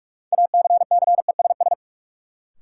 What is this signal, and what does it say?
Morse code, heard as a keyed tone, about 46 wpm, 700 Hz. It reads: ACCESS
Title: Sentences from Top 1000 Words - Review Part 4 - New Word Drill - 45wpm